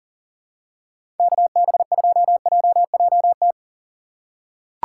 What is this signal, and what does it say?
Morse code, heard as a keyed tone, about 38 wpm, 700 Hz. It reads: KB2JJT
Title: USA Call Signs - 6 Characters - 3x Character Spacing - 40wpm